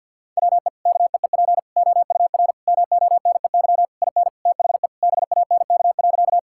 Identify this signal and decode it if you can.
Morse code, heard as a keyed tone, about 49 wpm, 700 Hz. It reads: WE KEEP OUR MONEY IN THE BANK.
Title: Sentences from Top 1000 Words - Review Part 6 - 50wpm